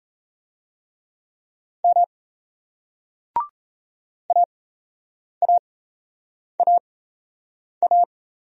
Morse code, heard as a keyed tone, about 35 wpm, spaced as for 8 wpm, 700 Hz. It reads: M AAAA
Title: Speed-Racing Single Letters 28wpm